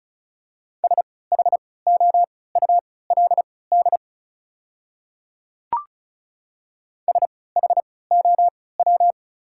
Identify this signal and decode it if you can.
Morse code, heard as a keyed tone, about 34 wpm, spaced as for 20 wpm, 700 Hz. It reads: SHOULD SHOW
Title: 3x Character Spacing - Top 300 Words - 35wpm